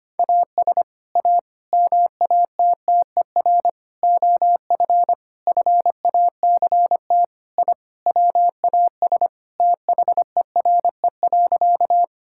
Morse code, heard as a keyed tone, about 25 wpm, 700 Hz. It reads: AS A MATTER OF FACT I WAS THERE.
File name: practice / 25wpm / Sentences with 4-Word Phrases - 25wpm